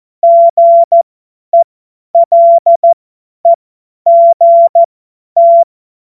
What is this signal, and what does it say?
Morse code, heard as a keyed tone, about 14 wpm, spaced as for 10 wpm, 680 Hz. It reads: GELEGT
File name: Morsekurs / Witze / Witz 02 Januar 2024 50 F70